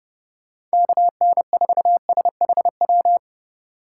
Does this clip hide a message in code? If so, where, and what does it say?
Yes: Morse code from 0.7 to 3.2 s, heard as a keyed tone, about 38 wpm, 700 Hz. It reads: KN4SHW